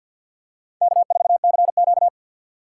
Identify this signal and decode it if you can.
Morse code, heard as a keyed tone, about 53 wpm, 700 Hz. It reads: K4CQ
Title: USA Call Signs - 4 Characters - 6x Character Spacing - 50wpm